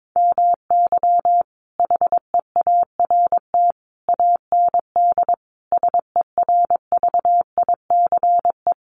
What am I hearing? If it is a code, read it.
Morse code, heard as a keyed tone, about 22 wpm, 700 Hz. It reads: MY HEART AND SERVICE
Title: Sentences from Top 1200 Words 22wpm